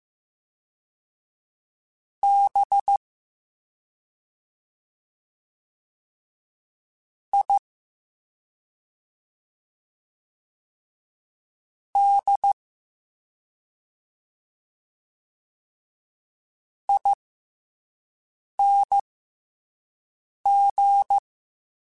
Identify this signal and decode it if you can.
Morse code, heard as a keyed tone, about 15 wpm, spaced as for 5 wpm, 800 Hz. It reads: B I D ING